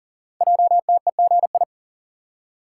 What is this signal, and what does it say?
Morse code, heard as a keyed tone, about 42 wpm, 700 Hz. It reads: JTEGI
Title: Code Groups - 5 Letters - 40wpm